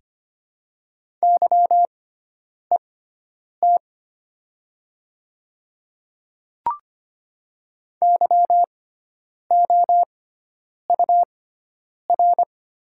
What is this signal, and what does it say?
Morse code, heard as a keyed tone, about 25 wpm, spaced as for 9 wpm, 700 Hz. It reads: YET YOUR
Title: 6x Character Spacing - Top 500 Words - 25wpm